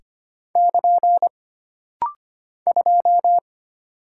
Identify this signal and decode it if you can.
Morse code, heard as a keyed tone, about 25 wpm, 700 Hz. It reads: <KN> 2
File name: Single Character (Letter:Number:Punctuation:Prosign) - Rapid-Fire - 25wpm